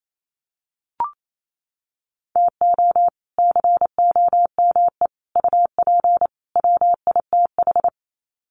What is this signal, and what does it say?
Morse code, heard as a keyed tone, about 28 wpm, 700 Hz. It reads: TO COME UP WITH
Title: Sets of 4 Words 28wpm